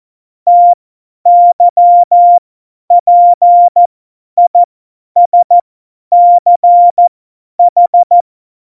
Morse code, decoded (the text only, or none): TYPISCH